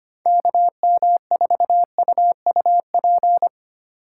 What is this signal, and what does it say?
Morse code, heard as a keyed tone, about 25 wpm, 700 Hz. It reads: KM4UUP
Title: USA Call Signs - 6 Characters - 4x Character Spacing - 25wpm